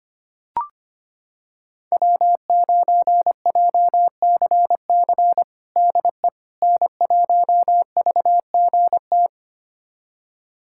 Morse code, heard as a keyed tone, about 25 wpm, 700 Hz. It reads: W9JCC DE N1VGT
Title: USA Callsign DE Callsign - 5 Characters - 25wpm